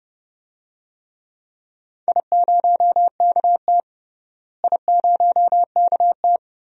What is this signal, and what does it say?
Morse code, heard as a keyed tone, about 30 wpm, 700 Hz. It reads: I0KT I0KT
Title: DXCC-100 Call Signs - 4 Characters - Repeated 3x - 30wpm